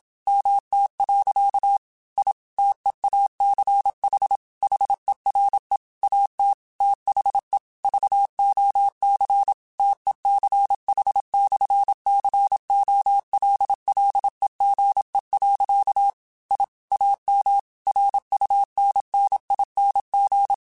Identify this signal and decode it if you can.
Morse code, heard as a keyed tone, about 27 wpm, 800 Hz. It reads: MT. I TEACH HERE AT THE VOC TECH/COLLEGE. I AM RUNNING
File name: QSOs / QSOs-mp3s / WD0FBT 1@25